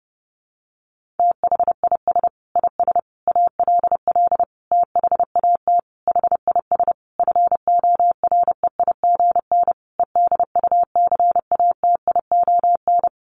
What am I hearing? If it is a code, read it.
Morse code, heard as a keyed tone, about 30 wpm, 700 Hz. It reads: THIS IS ALL THAT HIS FOREIGN EDUCATION
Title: Sentences from Top 1400 Words 30wpm